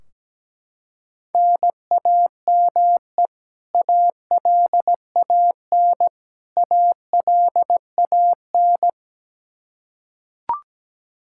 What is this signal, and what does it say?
Morse code, heard as a keyed tone, about 17 wpm, 700 Hz. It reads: NAME ALAN ALAN
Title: Practice QSO - Exchange 1 - Beginning - 17wpm